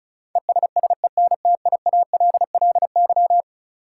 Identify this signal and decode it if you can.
Morse code, heard as a keyed tone, about 36 wpm, 700 Hz. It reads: ESSENTIALLY
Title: Sentences from Top 1300 Words - New Word Drill - 35wpm